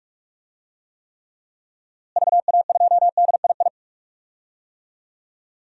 Morse code, heard as a keyed tone, about 45 wpm, 700 Hz. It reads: VA2DII
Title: DXCC-100 Call Signs - 6 Characters - Repeated 2x - 45wpm